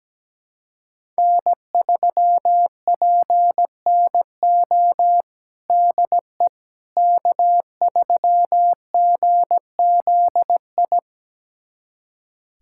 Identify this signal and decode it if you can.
Morse code, heard as a keyed tone, about 17 wpm, 700 Hz. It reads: N3PNO DE K3GZI
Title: USA Callsign DE Callsign - 5 Characters - 17wpm